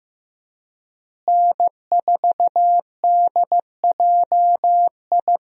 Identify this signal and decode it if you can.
Morse code, heard as a keyed tone, about 15 wpm, 700 Hz. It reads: N4DJI
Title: USA Call Signs - 5 Characters - Repeated 3x - 15wpm